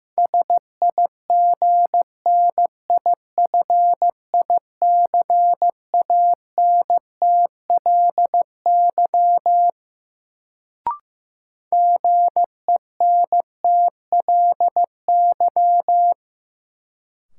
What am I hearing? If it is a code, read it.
Morse code, heard as a keyed tone, about 15 wpm, 700 Hz. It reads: SIGNIFICANTLY GENTLY
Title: Sentences from Top 1500 Words - New Word Drill - 15wpm